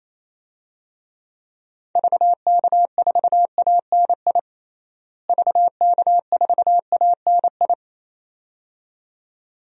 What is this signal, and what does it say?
Morse code, heard as a keyed tone, about 28 wpm, 700 Hz. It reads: VK4ANI VK4ANI
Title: DXCC-100 Call Signs - 6 Characters - Repeated 2x - 28wpm